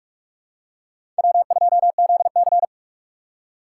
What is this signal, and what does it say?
Morse code, heard as a keyed tone, about 42 wpm, 700 Hz. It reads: W2ZC
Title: USA Call Signs - 4 Characters - 3x Character Spacing - 45wpm